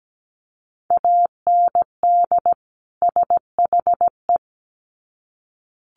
Morse code, heard as a keyed tone, about 17 wpm, 700 Hz. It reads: AND SHE
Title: Sets of 2 Words 17wpm